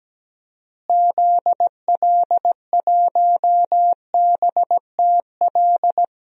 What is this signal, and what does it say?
Morse code, heard as a keyed tone, about 17 wpm, 700 Hz. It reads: ZL1BTL
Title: DXCC-100 Call Signs - 6 Characters - Repeated 3x - 17wpm